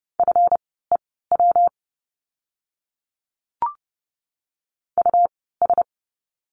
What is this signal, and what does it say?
Morse code, heard as a keyed tone, about 30 wpm, spaced as for 17 wpm, 700 Hz. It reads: FEW US
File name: course / 30:10wpm / 230 Lesson 37f - Letter x - Words 30wpm